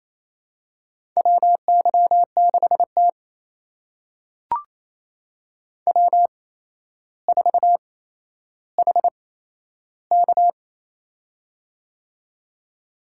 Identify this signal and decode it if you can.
Morse code, heard as a keyed tone, about 27 wpm, spaced as for 12 wpm, 700 Hz. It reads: WY6T W4HK
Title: USA Call Signs - 4 Characters - 8x Character Spacing - 28wpm